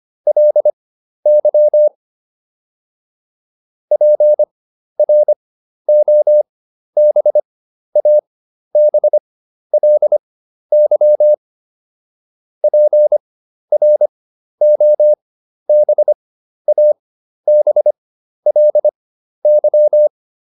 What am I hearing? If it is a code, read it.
Morse code, heard as a keyed tone, about 25 wpm, spaced as for 12 wpm, 600 Hz. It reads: LY PROBABLY PROBABLY